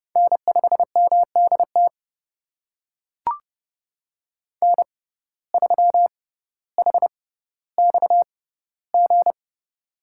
Morse code, heard as a keyed tone, about 31 wpm, spaced as for 16 wpm, 700 Hz. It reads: N5MDT N3HXG
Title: USA Call Signs - 5 Characters - 6x Character Spacing - 30wpm